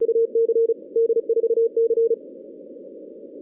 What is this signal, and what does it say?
Morse code, heard as a keyed tone, about 33 wpm, 460 Hz. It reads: UC D4C
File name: 2016 arrl dx cw / 2016-ARRL-DX-CW-K5TR- - 0-41